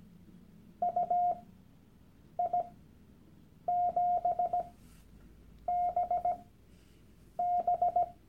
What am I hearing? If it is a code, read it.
Morse code, heard as a keyed tone, about 17 wpm, spaced as for 7 wpm, 710 Hz. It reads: UI7BB
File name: Workshop Dateien / CW-Online-Kurs / Lektion 001